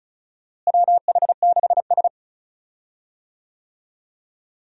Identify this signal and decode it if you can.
Morse code, heard as a keyed tone, about 36 wpm, 700 Hz. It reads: WH6S
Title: USA Call Signs - 4 Characters - Repeated 2x - 35wpm